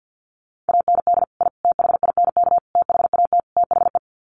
Morse code, heard as a keyed tone, about 50 wpm, 700 Hz. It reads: AND I THINK THAT THE
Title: Sets of 5 Words 50wpm